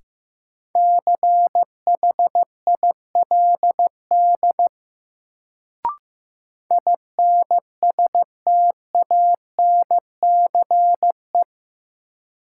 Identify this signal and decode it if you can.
Morse code, heard as a keyed tone, about 15 wpm, 700 Hz. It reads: CHILD INSTANCE
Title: Sentences from Top 800 Words - New Word Drill - 15wpm